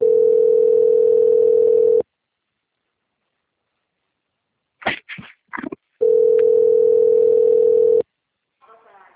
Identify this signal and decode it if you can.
Ringback tone (North America), 440 + 480 Hz.
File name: out- - - - - .47853